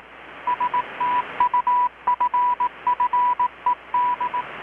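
Morse code, heard as a keyed tone, about 18 wpm, 980 Hz, with radio noise. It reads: STUFFED
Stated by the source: Morse from KPH's Night of Nights XX maritime radiotelegraphy event, received in Chiapas, Mexico about 3,800 km away on a Kenwood R-600 with a grounded 10 m long-wire antenna.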